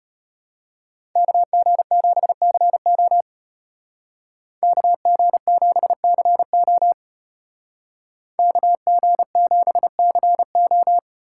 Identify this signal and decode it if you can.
Morse code, heard as a keyed tone, about 35 wpm, 700 Hz. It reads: KG7CO KG7CO KG7CO